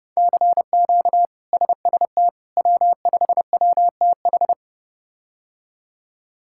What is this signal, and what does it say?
Morse code, heard as a keyed tone, about 30 wpm, 700 Hz. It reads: CQ SST W5WTH
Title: K1USN SST QSO Practice - 30wpm